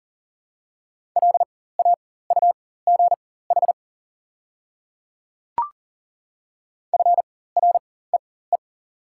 Morse code, heard as a keyed tone, about 40 wpm, spaced as for 19 wpm, 700 Hz. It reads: LAUGH FREE
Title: 4x Character Spacing - Top 500 Words - 40wpm